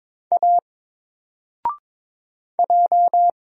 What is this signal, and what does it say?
Morse code, heard as a keyed tone, about 22 wpm, 700 Hz. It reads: A J
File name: Single-Letter-Number-22wpm